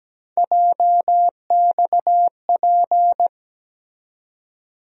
Morse code, heard as a keyed tone, about 17 wpm, 700 Hz. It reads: JXP
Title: USA Call Sign Postfixes - 3 Letters 17wpm